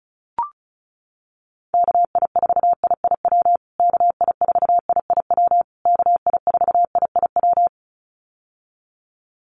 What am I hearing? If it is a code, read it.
Morse code, heard as a keyed tone, about 34 wpm, 700 Hz. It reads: KI4IIW KI4IIW KI4IIW
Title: USA Call Signs Repeated 3x 35wpm